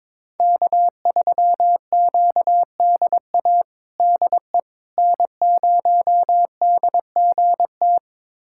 Morse code, heard as a keyed tone, about 21 wpm, 700 Hz. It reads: K3QDA DE N0DGT
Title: USA Callsign DE Callsign - 5 Characters - 22wpm